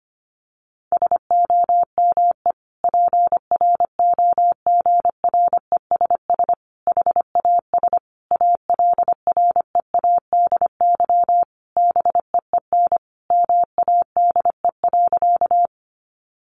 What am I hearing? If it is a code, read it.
Morse code, heard as a keyed tone, about 25 wpm, 700 Hz. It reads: SOME PROGRESS HAS ALREADY BEEN MADE.
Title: Sentences from Top 1400 Words 25wpm